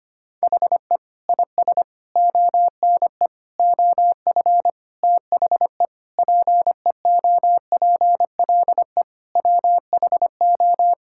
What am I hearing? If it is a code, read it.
Morse code, heard as a keyed tone, about 25 wpm, 700 Hz. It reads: HE IS ONE OF THE PEOPLE WHO
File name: practice / 25wpm / Sentences with 4-Word Phrases - 25wpm